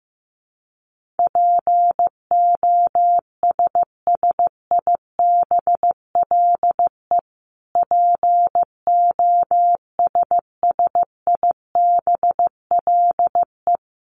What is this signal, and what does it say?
Morse code, heard as a keyed tone, about 15 wpm, 700 Hz. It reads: POSSIBLE POSSIBLE
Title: Top 500 Words Repeated 15wpm